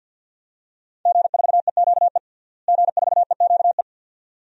Morse code, heard as a keyed tone, about 52 wpm, 700 Hz. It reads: G4EQE G4EQE
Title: DXCC-100 Call Signs - 5 Characters - Repeated 2x - 50wpm